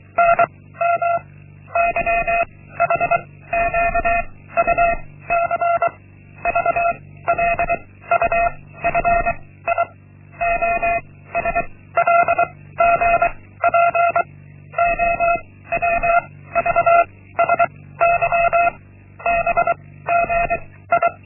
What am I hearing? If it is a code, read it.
Morse code, heard as a keyed tone, about 24 wpm, 690 Hz. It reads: N M Y H Q U C V L U F I O S L G P O W V S Y B G I
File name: Linksmakalnis-1